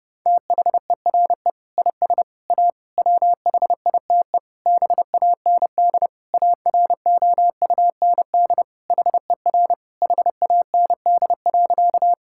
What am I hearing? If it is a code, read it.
Morse code, heard as a keyed tone, about 30 wpm, 700 Hz. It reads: THERE IS A WHITE BAND AROUND HER HAND.
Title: Sentences from Top 1600 Words 30wpm